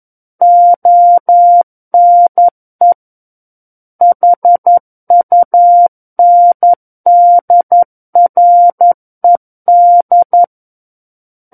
Morse code, heard as a keyed tone, about 11 wpm, 700 Hz. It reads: ONE HUNDRED
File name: Book One / 02-Chapter 2 - 11 WPM